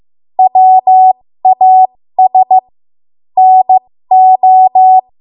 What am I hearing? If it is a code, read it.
Morse code, heard as a keyed tone, about 15 wpm, spaced as for 13 wpm, 750 Hz. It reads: WAS NO
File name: Morse / 13WPM